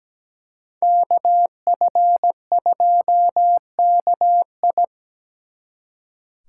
Morse code, heard as a keyed tone, about 20 wpm, 700 Hz. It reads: KF2KI
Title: Speed Racing - USA 2x2 Call Signs 17wpm